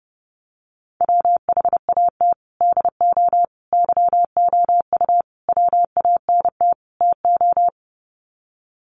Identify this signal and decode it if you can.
Morse code, heard as a keyed tone, about 30 wpm, 700 Hz. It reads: WHAT DO YOU WANT TO
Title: Sets of 5 Words 30wpm